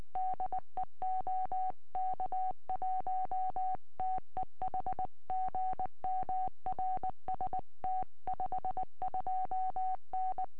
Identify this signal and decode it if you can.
Morse code, heard as a keyed tone, about 19 wpm, 750 Hz. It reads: DE OK1TEH GM RST 52N
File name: tropo / g3wzt 2m